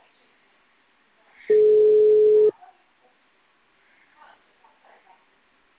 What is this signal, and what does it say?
Ringback tone (Europe, 425 Hz).